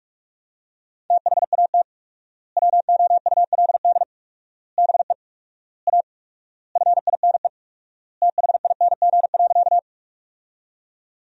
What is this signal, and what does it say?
Morse code, heard as a keyed tone, about 46 wpm, 700 Hz. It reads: THAT WOULD BE A FINE THING.
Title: 4x Word-Spacing - Sentences from Top 500 Words - 45wpm